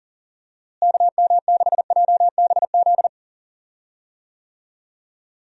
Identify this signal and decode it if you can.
Morse code, heard as a keyed tone, about 40 wpm, 700 Hz. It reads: KM6JBZ
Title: USA Call Signs - 6 Characters - Repeated 2x - 40wpm